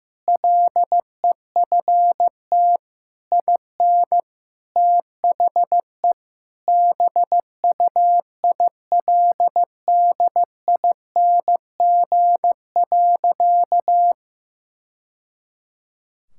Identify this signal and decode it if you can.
Morse code, heard as a keyed tone, about 16 wpm, 700 Hz. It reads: LEFT IN THE BUILDING.
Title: Sentences from Top 1000 Words - Review Part 5 - 15wpm